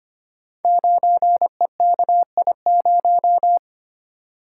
Morse code, heard as a keyed tone, about 25 wpm, 700 Hz. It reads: 9EKI0
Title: Code Groups - 5 Letter:Numbers - 25wpm